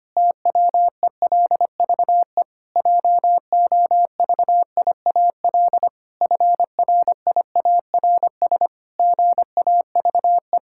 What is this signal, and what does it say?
Morse code, heard as a keyed tone, about 25 wpm, 700 Hz. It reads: TWELVE JOVIAL FRIARS GAVE
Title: Pangram Challenge v2 - 25wpm